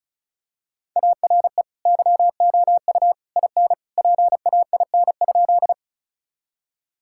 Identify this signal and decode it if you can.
Morse code, heard as a keyed tone, about 36 wpm, 700 Hz. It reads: ARE YOU IN PAIN?
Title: Sentences from Top 1600 Words 35wpm